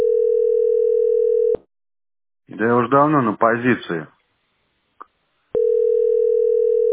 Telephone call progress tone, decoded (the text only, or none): ringback tone